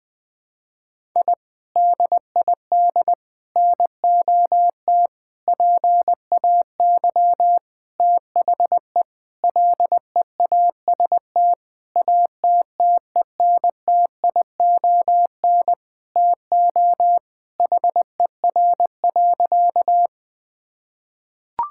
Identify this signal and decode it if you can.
Morse code, heard as a keyed tone, about 21 wpm, 700 Hz. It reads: I DID NOT PAY THE LEAST ATTENTION TO HER.
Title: Sentences from Top 1000 Words - Review Part 6 - 20wpm